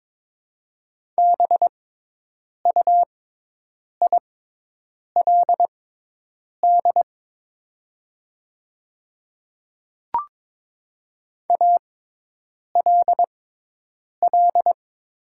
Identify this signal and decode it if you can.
Morse code, heard as a keyed tone, about 21 wpm, spaced as for 8 wpm, 700 Hz. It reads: BUILD ALL
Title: 6x Character Spacing - Top 200 Words - 22wpm